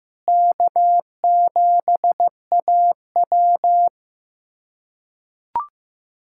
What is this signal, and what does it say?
Morse code, heard as a keyed tone, about 15 wpm, 700 Hz. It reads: K7AW
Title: USA Callsign DE Callsign - 4 Characters - 15wpm